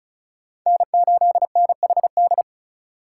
Morse code, heard as a keyed tone, about 45 wpm, 700 Hz. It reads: N8NHD